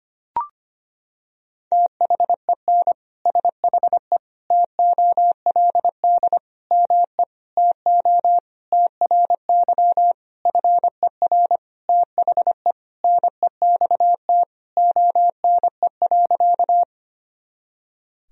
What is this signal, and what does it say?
Morse code, heard as a keyed tone, about 25 wpm, 700 Hz. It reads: THEN SHE TOLD ME TO TRY FER THE NEXT ONE.
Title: Sentences from Top 400 Words and Abbreviations 25wpm